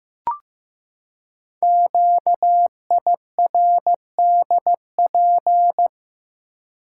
Morse code, heard as a keyed tone, about 15 wpm, 700 Hz. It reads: QIRDP
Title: Code Groups - 5 Letter:Numbers - 15wpm